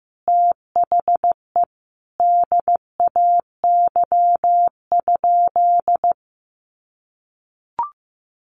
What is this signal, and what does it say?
Morse code, heard as a keyed tone, about 15 wpm, 700 Hz. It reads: THE DAY?
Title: Sentences from Top 1400 Words 15wpm